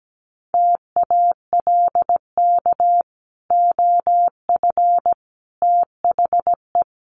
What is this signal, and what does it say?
Morse code, heard as a keyed tone, about 17 wpm, 700 Hz. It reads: TALK OF THE
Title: Sets of 5 Words 17wpm